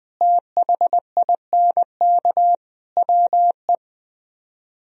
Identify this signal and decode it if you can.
Morse code, heard as a keyed tone, about 20 wpm, 700 Hz. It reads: THINK WE